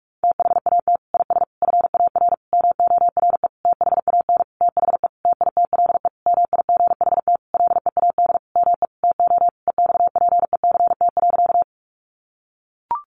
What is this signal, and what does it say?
Morse code, heard as a keyed tone, about 44 wpm, 700 Hz. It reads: THAT IS FAR MORE THAN THE TITLE MIGHT LEAD ME TO EXPECT.
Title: Sentences from Top 1200 Words 45wpm